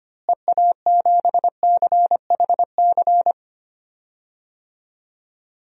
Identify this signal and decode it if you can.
Morse code, heard as a keyed tone, about 25 wpm, 700 Hz. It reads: EA7CHC
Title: DXCC-100 Call Signs - 6 Characters - Repeated 3x - 25wpm